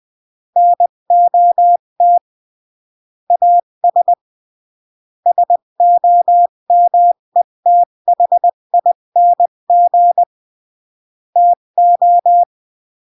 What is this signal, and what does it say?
Morse code, heard as a keyed tone, about 20 wpm, 700 Hz. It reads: NOT AS SOMETHING TO
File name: CW / jodorowsky / 82 commandments 20-15